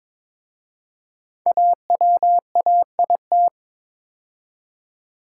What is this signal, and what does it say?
Morse code, heard as a keyed tone, about 22 wpm, 700 Hz. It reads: AWAIT
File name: 5 Letter Words - First 3 Letters Given - 22wpm